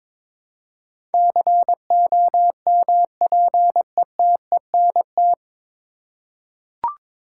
Morse code, heard as a keyed tone, about 22 wpm, 700 Hz. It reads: COMPETENT
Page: Morse Code Ninja practice 22wpm